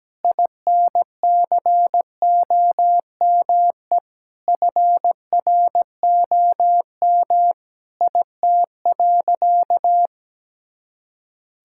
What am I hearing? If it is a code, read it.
Morse code, heard as a keyed tone, about 17 wpm, 700 Hz. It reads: INCOME FROM IT.
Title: Sentences from Top 1000 Words - Review Part 5 - 17wpm